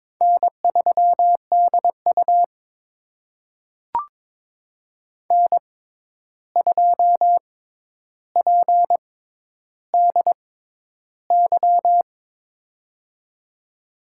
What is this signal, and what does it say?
Morse code, heard as a keyed tone, about 22 wpm, spaced as for 11 wpm, 700 Hz. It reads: N3DU N2PDY
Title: USA Call Signs - 6x Character Spacing - 22wpm